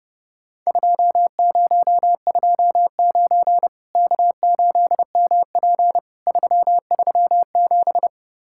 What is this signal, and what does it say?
Morse code, heard as a keyed tone, about 30 wpm, 700 Hz. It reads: 2029 K8MP 337
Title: Worked All Europe DX Contest - QTC Exchange - 30wpm